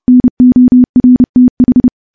Morse code, heard as a keyed tone, about 32 wpm, 250 Hz. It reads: NORTH
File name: morse / morse-training-30WPM